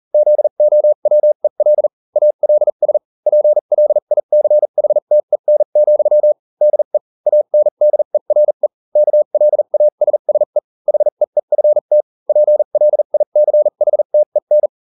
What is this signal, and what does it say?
Morse code, heard as a keyed tone, about 40 wpm, spaced as for 35 wpm, 600 Hz. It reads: ZOWEL ALS PLICHTEN, DE ANDERE KLASSE HEEFT PLICHTEN